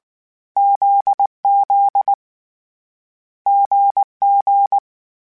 Morse code, heard as a keyed tone, about 19 wpm, 800 Hz. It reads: ZZ GG
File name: Difficult Combos / Difficult Combos-mp3s / GZ GZ7 @18